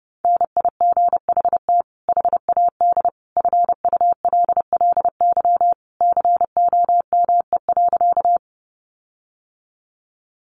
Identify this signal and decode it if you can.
Morse code, heard as a keyed tone, about 30 wpm, 700 Hz. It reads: NIGHT HAD FULLY COME.
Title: Sentences from Top 1300 Words 30wpm